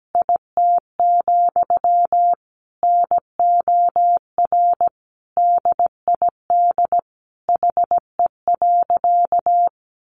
Morse code, heard as a keyed tone, about 17 wpm, 700 Hz. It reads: IT, NOR DID HE.
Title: Sentences from Top 1500 Words 17wpm